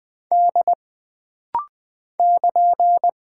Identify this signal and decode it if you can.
Morse code, heard as a keyed tone, about 20 wpm, 700 Hz. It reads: D <KN>
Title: Single Character (Letter:Number:Punctuation:Prosign) - Rapid-Fire - 20wpm